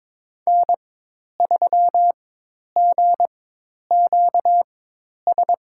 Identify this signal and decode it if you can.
Morse code, heard as a keyed tone, about 22 wpm, spaced as for 10 wpm, 700 Hz. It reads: N3GQS
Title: USA Call Signs - 5 Characters - 4x Character Spacing - 22wpm